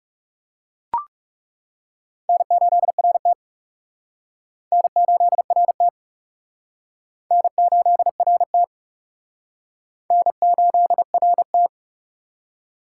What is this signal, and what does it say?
Morse code, heard as a keyed tone, about 36 wpm, 700 Hz. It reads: N8RT N8RT N8RT N8RT
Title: Speed Racing - USA Call Signs 30wpm